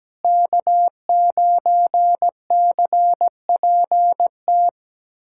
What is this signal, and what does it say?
Morse code, heard as a keyed tone, about 17 wpm, 700 Hz. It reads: K9CPT
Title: USA Callsign DE Callsign - 5 Characters - 17wpm